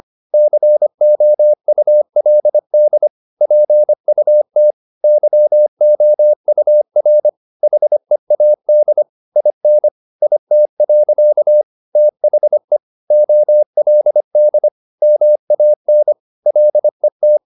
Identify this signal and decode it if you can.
Morse code, heard as a keyed tone, about 25 wpm, 600 Hz. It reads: COULD PUT YOUR HEAD IN IT. THE OLD MAN LET